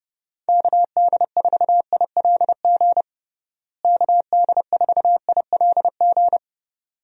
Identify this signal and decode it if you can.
Morse code, heard as a keyed tone, about 30 wpm, 700 Hz. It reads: KD4ILG KD4ILG
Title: USA Call Signs - 6 Characters - Repeated 2x - 30wpm